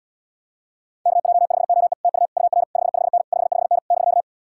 Morse code, heard as a keyed tone, about 80 wpm, 700 Hz. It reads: KJ5OE TU UR 55N 55N <BK>